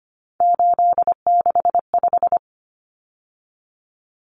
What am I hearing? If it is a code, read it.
Morse code, heard as a keyed tone, about 25 wpm, 700 Hz. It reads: 865
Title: Common Cut Numbers - 3-Digit 25wpm